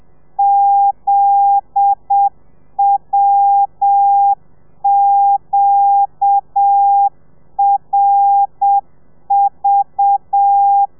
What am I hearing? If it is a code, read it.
Morse code, heard as a keyed tone, about 7 wpm, 800 Hz. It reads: ZWQRV